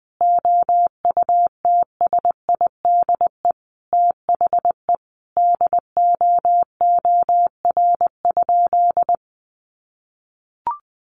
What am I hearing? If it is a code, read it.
Morse code, heard as a keyed tone, about 20 wpm, 700 Hz. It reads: OUTSIDE THE DOOR?
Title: Sentences from Top 900 Words 20wpm